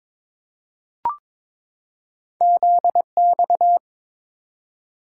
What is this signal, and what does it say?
Morse code, heard as a keyed tone, about 22 wpm, 700 Hz. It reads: ZX
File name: practice / 22wpm / Two Letter-Number Combinations - 22wpm